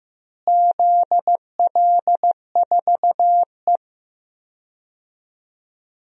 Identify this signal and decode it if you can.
Morse code, heard as a keyed tone, about 15 wpm, 700 Hz. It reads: ZL4E
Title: DXCC-100 Call Signs - 4 Characters - Repeated 2x - 15wpm